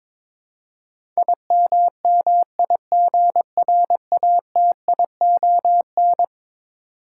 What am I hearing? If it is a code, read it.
Morse code, heard as a keyed tone, about 22 wpm, 700 Hz. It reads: IMMIGRATION
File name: practice / 22wpm / Eleven Letter Words - 22wpm